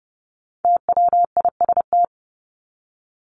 Morse code, heard as a keyed tone, about 30 wpm, 700 Hz. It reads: TWIST
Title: Top 1751-2000 Words 30wpm